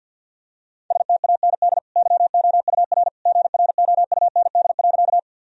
Morse code, heard as a keyed tone, about 49 wpm, 700 Hz. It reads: STAND YOUR GROUND.
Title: Sentences from Top 1000 Words - Review Part 2 - 50wpm